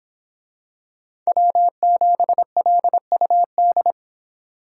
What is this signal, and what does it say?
Morse code, heard as a keyed tone, about 23 wpm, 700 Hz. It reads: W7LUD